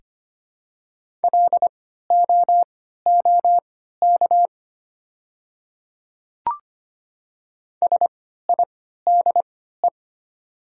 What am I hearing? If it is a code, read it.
Morse code, heard as a keyed tone, about 25 wpm, spaced as for 14 wpm, 700 Hz. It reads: LOOK SIDE
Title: 3x Character Spacing - Top 200 Words - 25wpm